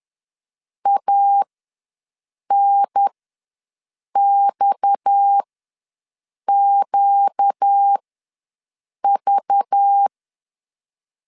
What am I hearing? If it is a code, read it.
Morse code, heard as a keyed tone, about 10 wpm, spaced as for 6 wpm, 800 Hz. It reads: ANXQV
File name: LEZIONI / 7 GF9Z / 7 GF ws 10wpm 3918